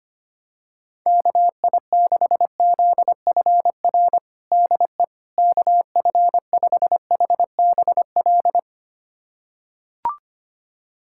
Morse code, heard as a keyed tone, about 25 wpm, 700 Hz. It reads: KI6ZFR DE KF5HBL
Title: USA Callsign DE Callsign - 6 Characters - 25wpm